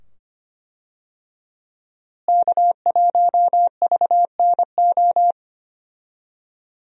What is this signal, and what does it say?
Morse code, heard as a keyed tone, about 25 wpm, 700 Hz. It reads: K1VNO
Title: USA Call Signs - 5 Characters - Repeated 2x - 25wpm